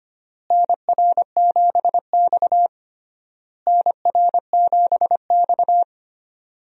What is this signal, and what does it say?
Morse code, heard as a keyed tone, about 25 wpm, 700 Hz. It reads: NR7X NR7X
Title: USA Call Signs - 4 Characters - Repeated 3x - 25wpm